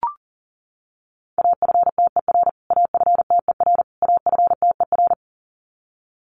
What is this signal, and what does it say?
Morse code, heard as a keyed tone, about 40 wpm, 700 Hz. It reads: AFTER AFTER AFTER
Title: Top 300 Words Repeated 40wpm